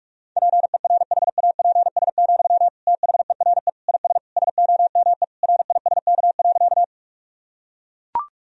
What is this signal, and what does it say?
Morse code, heard as a keyed tone, about 41 wpm, 700 Hz. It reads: PERHAPS, THERE IS SOME RISK.
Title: Sentences from Top 1000 Words - Review Part 4 - 45wpm